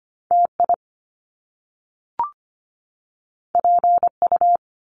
Morse code, heard as a keyed tone, about 25 wpm, 700 Hz. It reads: TI PU
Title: Top Two Letter Combinations in Words 25wpm